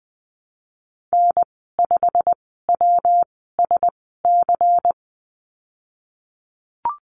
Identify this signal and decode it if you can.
Morse code, heard as a keyed tone, about 20 wpm, spaced as for 14 wpm, 700 Hz. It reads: N5WSC
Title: 154 Lesson 25g - Letter p - Call Signs 20wpm